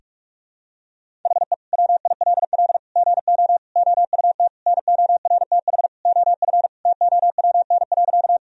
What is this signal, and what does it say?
Morse code, heard as a keyed tone, about 42 wpm, 700 Hz. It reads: HE WILL GO OUT NORTH OF TOWN.
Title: Sentences from Top 1000 Words - Review Part 1 - 45wpm